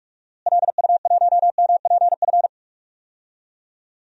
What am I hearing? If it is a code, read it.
Morse code, heard as a keyed tone, about 46 wpm, 700 Hz. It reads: LU1MPF
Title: DXCC-100 Call Signs - 6 Characters - Repeated 2x - 45wpm